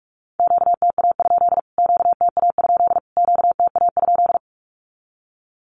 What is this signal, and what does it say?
Morse code, heard as a keyed tone, about 44 wpm, 700 Hz. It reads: QTA? QTA? QTA?
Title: Q-Codes Repeated 3x 45wpm